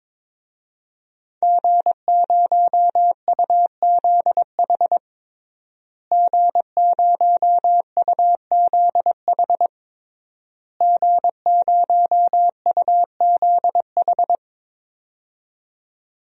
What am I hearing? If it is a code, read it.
Morse code, heard as a keyed tone, about 22 wpm, 700 Hz. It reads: G0UZH G0UZH G0UZH
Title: DXCC-100 Call Signs - 5 Characters - Repeated 3x - 22wpm